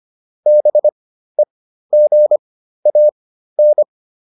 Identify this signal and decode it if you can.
Morse code, heard as a keyed tone, about 25 wpm, spaced as for 13 wpm, 600 Hz. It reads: BEGAN